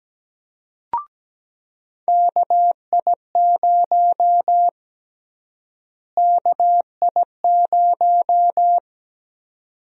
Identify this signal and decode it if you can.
Morse code, heard as a keyed tone, about 17 wpm, 700 Hz. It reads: KI0 KI0
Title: USA Call Sign Prefixes - Repeated 2x - 17wpm